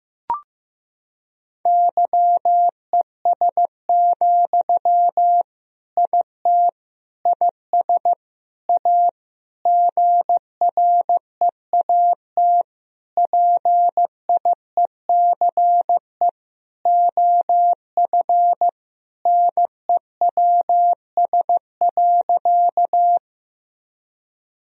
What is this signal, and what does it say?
Morse code, heard as a keyed tone, about 15 wpm, 700 Hz. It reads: YES, IT IS A GREAT PIECE OF NEWS.
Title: Sentences from Top 1000 Words - Review Part 2 - 15wpm